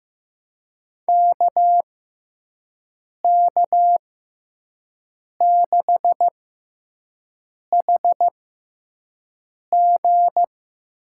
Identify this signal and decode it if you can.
Morse code, heard as a keyed tone, about 15 wpm, spaced as for 5 wpm, 700 Hz. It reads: KK6HG
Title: USA Call Signs - 6 Characters - 6x Character Spacing - 15wpm